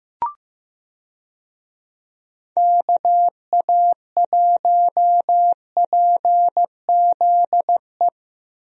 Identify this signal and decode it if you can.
Morse code, heard as a keyed tone, about 15 wpm, 700 Hz. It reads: KA1PZE
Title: USA Call Signs - 6 Characters - Repeated 3x - 15wpm